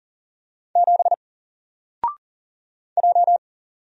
Morse code, heard as a keyed tone, about 40 wpm, 700 Hz. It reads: 7 J
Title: Single Letter-Number Alt 40wpm